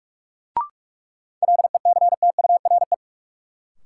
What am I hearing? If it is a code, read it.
Morse code, heard as a keyed tone, about 45 wpm, 700 Hz. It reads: LECTURE
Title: Sentences from Top 1600 Words - New Word Drill Alt - 45wpm